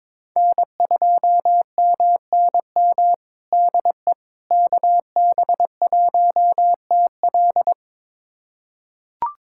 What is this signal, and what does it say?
Morse code, heard as a keyed tone, about 22 wpm, 700 Hz. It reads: N2MNM DE KB1TL
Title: USA Callsign DE Callsign - 5 Characters - 22wpm